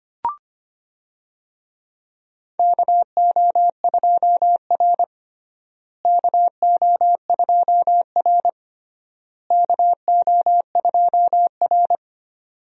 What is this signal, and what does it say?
Morse code, heard as a keyed tone, about 25 wpm, 700 Hz. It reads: KO2R KO2R KO2R
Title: USA Call Signs - 4 Characters - Repeated 3x - 25wpm